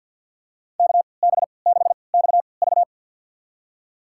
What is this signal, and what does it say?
Morse code, heard as a keyed tone, about 50 wpm, spaced as for 28 wpm, 700 Hz. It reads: KB6XV
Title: USA Call Signs - 3x Character Spacing - 50wpm